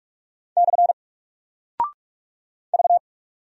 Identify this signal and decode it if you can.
Morse code, heard as a keyed tone, about 44 wpm, 700 Hz. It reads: / V
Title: Single Character (Letter:Number:Punctuation:Prosign) 45wpm